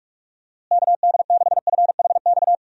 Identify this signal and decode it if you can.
Morse code, heard as a keyed tone, about 43 wpm, 700 Hz. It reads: KD6FHX
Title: USA Call Signs - 6 Characters - Repeated 2x - 45wpm